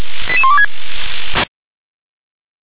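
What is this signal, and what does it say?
Five-tone selective call (ZVEI), heard as quick runs of five tones, tones of about 70 ms. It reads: 88126
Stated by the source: Five-tone ZVEI selective call received on SDR.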